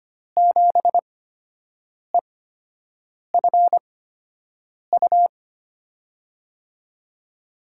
Morse code, heard as a keyed tone, about 25 wpm, spaced as for 7 wpm, 700 Hz. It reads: 7EFU
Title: USA Call Signs - 6 Characters - 8x Character Spacing - 25wpm